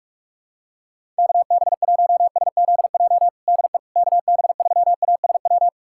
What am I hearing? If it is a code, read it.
Morse code, heard as a keyed tone, about 46 wpm, 700 Hz. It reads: KB1SZJ DE KB3ASW
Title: USA Callsign DE Callsign - 6 Characters - 45wpm